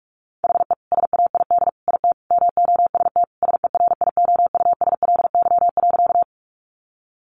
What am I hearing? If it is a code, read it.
Morse code, heard as a keyed tone, about 44 wpm, 700 Hz. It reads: HE SAID IT MOST SERIOUSLY.
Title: Sentences from Top 1200 Words 45wpm